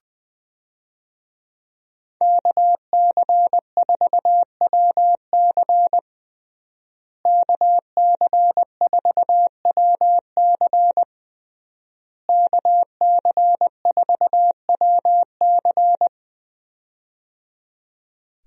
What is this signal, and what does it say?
Morse code, heard as a keyed tone, about 20 wpm, 700 Hz. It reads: KC4WC KC4WC KC4WC
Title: USA Call Signs - 5 Characters - Repeated 3x - 20wpm